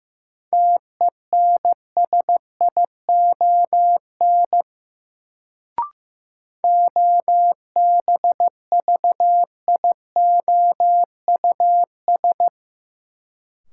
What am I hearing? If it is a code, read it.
Morse code, heard as a keyed tone, about 15 wpm, 700 Hz. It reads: TENSION OBVIOUS
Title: Sentences from Top 1600 Words - New Word Drill Alt - 15wpm